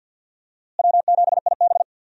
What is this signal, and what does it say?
Morse code, heard as a keyed tone, about 50 wpm, 700 Hz. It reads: W7IB
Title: USA Call Signs - 4 Characters - Repeated 2x - 50wpm